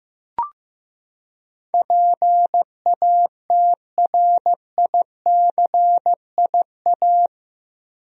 Morse code, heard as a keyed tone, about 15 wpm, 700 Hz. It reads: PATRICIA
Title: USA Common Names - 15wpm